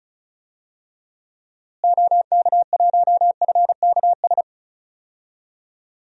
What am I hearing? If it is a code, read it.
Morse code, heard as a keyed tone, about 36 wpm, 700 Hz. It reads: OK1FKS
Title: DXCC-100 Call Signs - 6 Characters - Repeated 2x - 35wpm